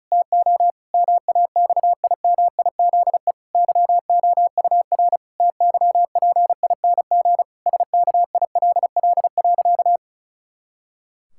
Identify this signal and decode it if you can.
Morse code, heard as a keyed tone, about 36 wpm, 700 Hz. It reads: TO MAXIMIZE YOUR TYPING SKILL.
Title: Pangram Challenge v2 - 35wpm